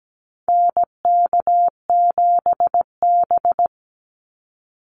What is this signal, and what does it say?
Morse code, heard as a keyed tone, about 17 wpm, 700 Hz. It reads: NK7B
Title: Call Signs - 17wpm